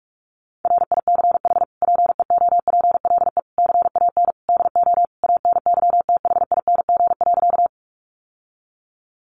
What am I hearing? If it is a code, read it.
Morse code, heard as a keyed tone, about 44 wpm, 700 Hz. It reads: RICH PEOPLE CAN DO ANYTHING.
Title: Sentences from Top 1100 Words 45wpm